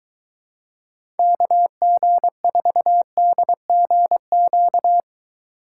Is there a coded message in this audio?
Yes: Morse code from 1.2 to 5.0 s, heard as a keyed tone, about 22 wpm, 700 Hz. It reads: KG4DGQ